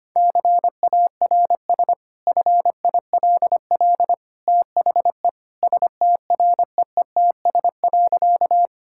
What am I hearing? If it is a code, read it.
Morse code, heard as a keyed tone, about 25 wpm, 690 Hz. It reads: CARS FILL THE STREETS.
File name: Easy Sentences with Hints - 25wpm